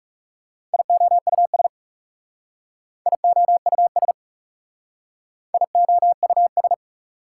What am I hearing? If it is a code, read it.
Morse code, heard as a keyed tone, about 37 wpm, 690 Hz. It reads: IOUS IOUS IOUS
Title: Speed-Racing Prefixes and Suffixes 30wpm